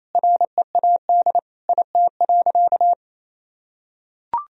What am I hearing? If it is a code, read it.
Morse code, heard as a keyed tone, about 28 wpm, 700 Hz. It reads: READ IT.
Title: Sentences from Top 500 Words and Abbreviations 28wpm